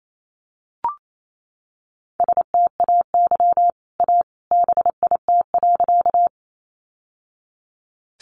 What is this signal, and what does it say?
Morse code, heard as a keyed tone, about 28 wpm, 700 Hz. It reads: STAY A BIT.
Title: Sentences from Top 900 Words 28wpm